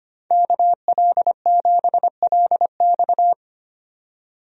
Morse code, heard as a keyed tone, about 25 wpm, 700 Hz. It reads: KL7LX
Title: DXCC-100 Call Signs - 5 Characters - 25wpm